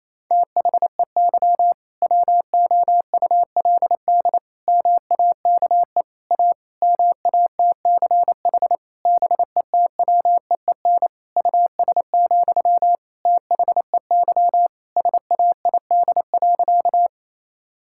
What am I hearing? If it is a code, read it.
Morse code, heard as a keyed tone, about 28 wpm, 700 Hz. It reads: THEY WOULD MAKE A MATCH BETWEEN US, THEY SAID.
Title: Sentences from Top 1600 Words 28wpm